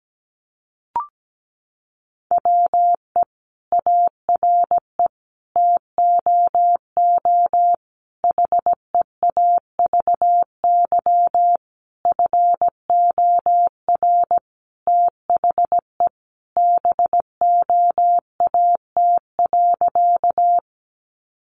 Morse code, heard as a keyed tone, about 17 wpm, 700 Hz. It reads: WE ARE TOO HEAVY FOR THE BOAT.
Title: Sentences from Top 1100 Words 17wpm